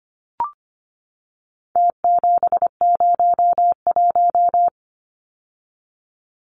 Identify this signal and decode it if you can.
Morse code, heard as a keyed tone, about 24 wpm, 700 Hz. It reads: T701
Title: Numbers - Contest Style - 4-Digit 25wpm